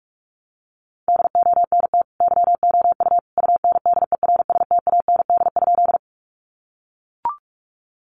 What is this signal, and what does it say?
Morse code, heard as a keyed tone, about 44 wpm, 700 Hz. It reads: DONT YOU UNDERSTAND?
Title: Sentences from Top 600 Words 45wpm